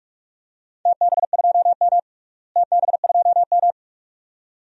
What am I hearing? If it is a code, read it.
Morse code, heard as a keyed tone, about 46 wpm, 700 Hz. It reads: TB2M TB2M
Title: DXCC-100 Call Signs - 4 Characters - Repeated 2x - 45wpm